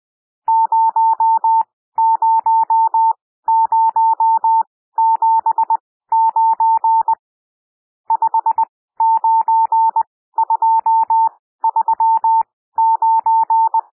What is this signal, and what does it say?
Morse code, heard as a keyed tone, about 20 wpm, spaced as for 14 wpm, 900 Hz. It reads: 00079 59239
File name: cwmp3 / N1 SOUND NUMBERS 20 wpm